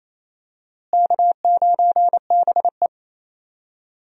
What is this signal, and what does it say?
Morse code, heard as a keyed tone, about 27 wpm, 700 Hz. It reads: K9BE
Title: USA Call Signs - 4 Characters - Repeated 3x - 28wpm